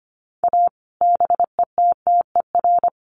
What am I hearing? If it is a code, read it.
Morse code, heard as a keyed tone, about 25 wpm, 700 Hz. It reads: A BETTER
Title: Sets of 5 Words Encore2 25wpm